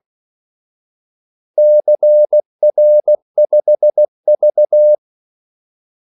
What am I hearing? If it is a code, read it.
Morse code, heard as a keyed tone, about 16 wpm, 600 Hz. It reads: CR5V
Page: Kaldesignaler 16 wpm